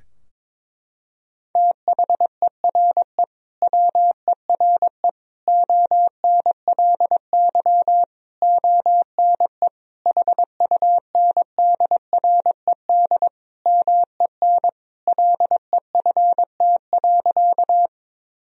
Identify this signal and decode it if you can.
Morse code, heard as a keyed tone, about 22 wpm, 700 Hz. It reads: THERE WERE ONLY ONE HUNDRED MEN LEFT.
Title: Sentences from Top 1000 Words - Review Part 2 - 22wpm